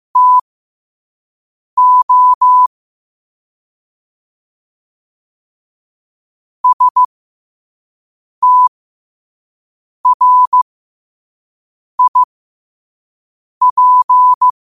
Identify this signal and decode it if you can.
Morse code, heard as a keyed tone, about 15 wpm, spaced as for 5 wpm, 1000 Hz. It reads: TO STRIP